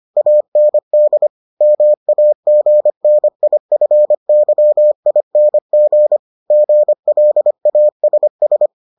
Morse code, heard as a keyed tone, about 25 wpm, 600 Hz. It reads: AND MAGNIFYING GLASS